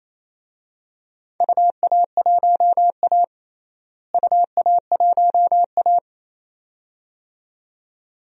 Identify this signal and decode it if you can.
Morse code, heard as a keyed tone, about 27 wpm, 700 Hz. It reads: UA1A UA1A
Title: DXCC-100 Call Signs - 4 Characters - Repeated 2x - 28wpm